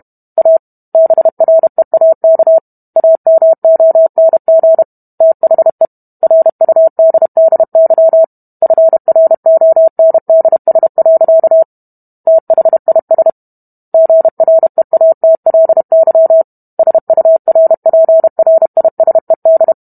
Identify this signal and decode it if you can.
Morse code, heard as a keyed tone, about 32 wpm, 650 Hz. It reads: A BREAK AMONG THE RUDDY FRONDS. THIS GREATLY SURPRISED